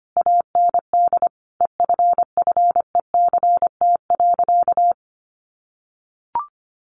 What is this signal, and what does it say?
Morse code, heard as a keyed tone, about 25 wpm, 700 Hz. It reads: AND EFFECT.
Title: Sentences from Top 1000 Words 25wpm